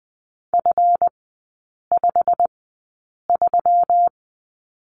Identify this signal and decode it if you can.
Morse code, heard as a keyed tone, about 20 wpm, spaced as for 8 wpm, 700 Hz. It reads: F53
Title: 138 Lesson 23e - Letter f - Three Characters 20wpm